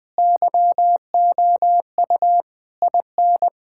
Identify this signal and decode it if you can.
Morse code, heard as a keyed tone, about 20 wpm, 700 Hz. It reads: YOU IN
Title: Sentences from Top 1600 Words 20wpm